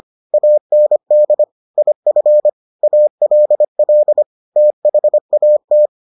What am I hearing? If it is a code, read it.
Morse code, heard as a keyed tone, about 25 wpm, 600 Hz. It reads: AND IF ALL THAT